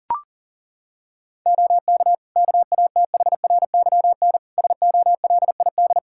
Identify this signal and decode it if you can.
Morse code, heard as a keyed tone, about 40 wpm, 700 Hz. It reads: OK KATHRYN SOLID
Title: K1USN SST QSO Practice 40wpm